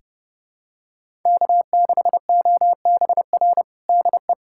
Morse code, heard as a keyed tone, about 32 wpm, 700 Hz. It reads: K6OBR DE
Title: USA Callsign DE Callsign 30wpm